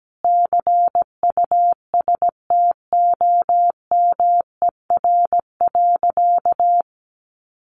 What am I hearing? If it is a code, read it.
Morse code, heard as a keyed tone, about 17 wpm, 700 Hz. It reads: CUSTOMER.
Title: Sentences from Top 1100 Words 17wpm